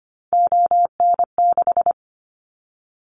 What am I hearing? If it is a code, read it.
Morse code, heard as a keyed tone, about 25 wpm, 700 Hz. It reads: ON6
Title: Callsign Prefixes for Top 25 DXCC - 25wpm